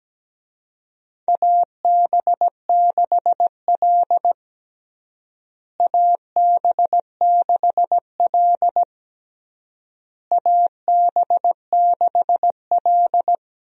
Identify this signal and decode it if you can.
Morse code, heard as a keyed tone, about 17 wpm, 700 Hz. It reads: AB6L AB6L AB6L
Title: USA Call Signs - 4 Characters - Repeated 3x - 17wpm